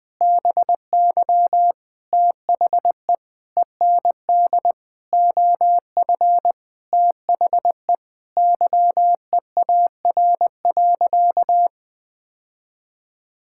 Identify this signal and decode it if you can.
Morse code, heard as a keyed tone, about 20 wpm, 700 Hz. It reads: BY THE END OF THE YEAR.
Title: Sentences with 5-Word Phrases - 20wpm